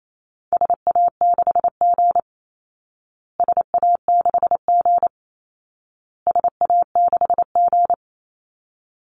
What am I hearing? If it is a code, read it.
Morse code, heard as a keyed tone, about 28 wpm, 700 Hz. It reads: SA6G SA6G SA6G
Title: Call Signs Repeated 2x - 28wpm